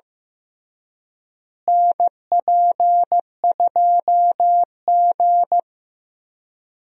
Morse code, heard as a keyed tone, about 15 wpm, 700 Hz. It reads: NP2G
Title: USA Call Signs - 4 Characters - Repeated 3x - 15wpm